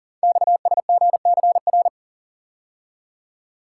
Morse code, heard as a keyed tone, about 38 wpm, 700 Hz. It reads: XSGCR
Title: Code Groups - 5 Letters - 40wpm